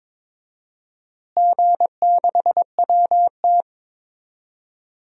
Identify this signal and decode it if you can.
Morse code, heard as a keyed tone, about 22 wpm, 700 Hz. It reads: G6WT
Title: DXCC-100 Call Signs - 4 Characters - Repeated 2x - 22wpm